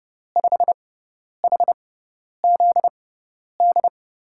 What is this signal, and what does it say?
Morse code, heard as a keyed tone, about 31 wpm, spaced as for 10 wpm, 700 Hz. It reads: 5HZD
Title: USA Call Signs - 6 Characters - 6x Character Spacing - 30wpm